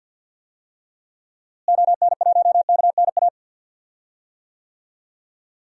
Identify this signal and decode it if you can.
Morse code, heard as a keyed tone, about 50 wpm, 700 Hz. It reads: ON1KNA
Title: DXCC-100 Call Signs - 6 Characters - Repeated 3x - 50wpm